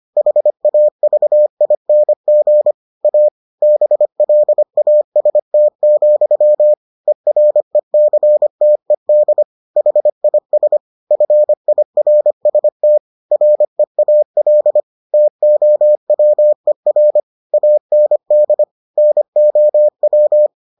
Morse code, heard as a keyed tone, about 25 wpm, 600 Hz. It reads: HAVING A BLAST, ERECTED HIS FIRST REAL TOWER AND NOW